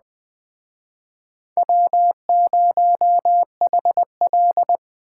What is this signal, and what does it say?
Morse code, heard as a keyed tone, about 19 wpm, 700 Hz. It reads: W0HL